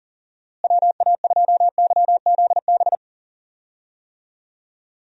Morse code, heard as a keyed tone, about 39 wpm, 700 Hz. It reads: WA2YZB
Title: USA Call Signs - 6 Characters - Repeated 3x - 40wpm